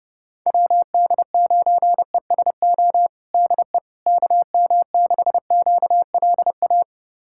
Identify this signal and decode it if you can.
Morse code, heard as a keyed tone, about 32 wpm, 700 Hz. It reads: WD9ESO DE KM6QLA
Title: USA Callsign DE Callsign 30wpm